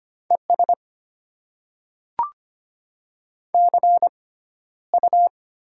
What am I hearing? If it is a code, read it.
Morse code, heard as a keyed tone, about 25 wpm, spaced as for 13 wpm, 700 Hz. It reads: ES CU
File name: Top 20 QSO Elements - 6x Character Spacing - 25wpm